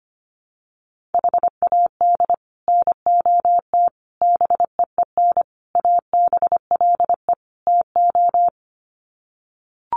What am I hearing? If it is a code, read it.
Morse code, heard as a keyed tone, about 25 wpm, 700 Hz. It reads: HAD NOT BEEN ABLE TO
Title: Sets of 5 Words Encore 25wpm